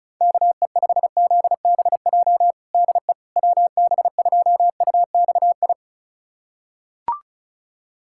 Morse code, heard as a keyed tone, about 34 wpm, 700 Hz. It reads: KE5ZBJ DE WB2UXI
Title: USA Callsign DE Callsign 35wpm